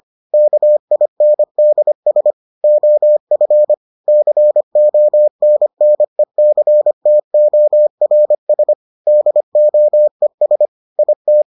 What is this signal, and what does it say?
Morse code, heard as a keyed tone, about 25 wpm, 600 Hz. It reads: KINDS OF CONNECTORS DOES IT